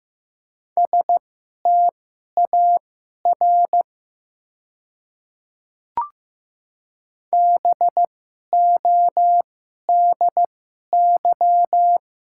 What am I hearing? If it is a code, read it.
Morse code, heard as a keyed tone, about 15 wpm, spaced as for 11 wpm, 700 Hz. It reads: STAR BODY
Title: 2x Character Spacing - Top 500 Words - 15wpm